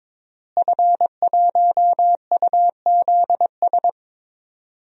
Morse code, heard as a keyed tone, about 22 wpm, 700 Hz. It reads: F1UZS
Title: DXCC-100 Call Signs - 5 Characters - Repeated 3x - 22wpm